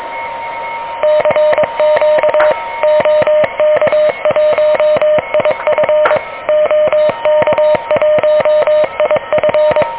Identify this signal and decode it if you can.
Morse code, heard as a keyed tone, about 23 wpm, 600 Hz, with noise. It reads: CQ OK1IF OK1IF
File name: OK QRP 2023 / OK QRP 2023